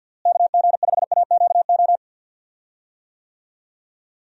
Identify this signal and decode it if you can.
Morse code, heard as a keyed tone, about 52 wpm, 700 Hz. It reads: KG5AQO
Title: USA Call Signs - 6 Characters - Repeated 3x - 50wpm